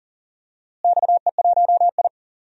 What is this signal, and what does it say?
Morse code, heard as a keyed tone, about 40 wpm, 700 Hz. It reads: XE1I
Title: DXCC-100 Call Signs - 4 Characters - Repeated 3x - 40wpm